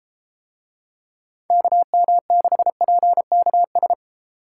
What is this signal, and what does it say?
Morse code, heard as a keyed tone, about 32 wpm, 700 Hz. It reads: KM6PKS